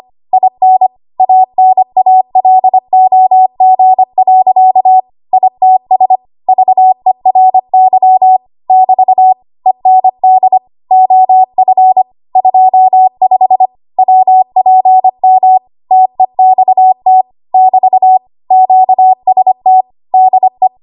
Morse code, heard as a keyed tone, about 25 wpm, 750 Hz. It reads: IN ANALOG. ITS VERY <BT> END OF 25 WPM TEXT <BT> QST DE
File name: Archive / 25 WPM / 25WPM